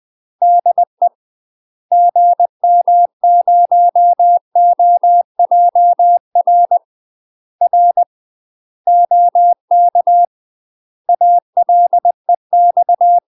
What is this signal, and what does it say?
Morse code, heard as a keyed tone, about 20 wpm, 700 Hz. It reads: DE GM0OJR R OK ALEX
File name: UK MASTER QSOs TRAINING / 20WPM / UK MASTER QSOs 0046